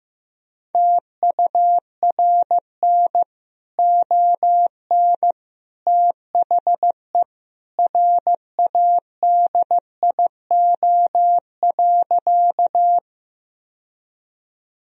Morse code, heard as a keyed tone, about 15 wpm, 700 Hz. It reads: TURN ON THE RADIO.
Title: Sentences from Top 1000 Words - Review Part 3 - 15wpm